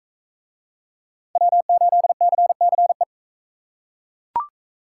Morse code, heard as a keyed tone, about 38 wpm, 700 Hz. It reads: W8CCE